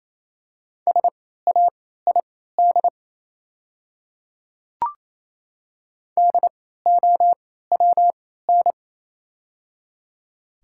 Morse code, heard as a keyed tone, about 28 wpm, spaced as for 16 wpm, 700 Hz. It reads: SAID DOWN
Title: 3x Character Spacing - Top 100 Words - 28wpm